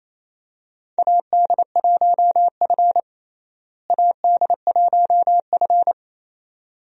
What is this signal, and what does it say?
Morse code, heard as a keyed tone, about 28 wpm, 700 Hz. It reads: AD1F AD1F
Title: USA Call Signs - 4 Characters - Repeated 2x - 28wpm